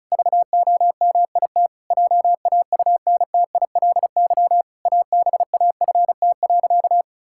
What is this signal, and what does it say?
Morse code, heard as a keyed tone, about 34 wpm, 700 Hz. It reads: VOMIT JAUNTILY ABAFT.
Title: Pangram Challenge v2 - 35wpm